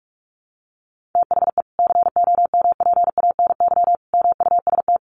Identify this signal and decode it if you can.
Morse code, heard as a keyed tone, about 44 wpm, 700 Hz. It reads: THE COMPANY MUST
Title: Sentences from Top 1400 Words 45wpm